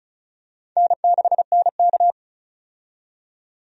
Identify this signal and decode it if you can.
Morse code, heard as a keyed tone, about 36 wpm, 700 Hz. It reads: N6NK
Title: USA Call Signs - 4 Characters - 35wpm